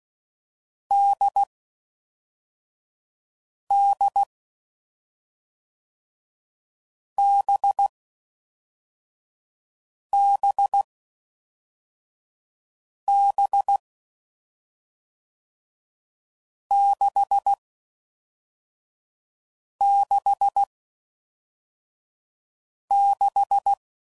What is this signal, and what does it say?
Morse code, heard as a keyed tone, about 16 wpm, 800 Hz. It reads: D D B B B 6 6 6
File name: Mixed / Short Runs / NDB NDB6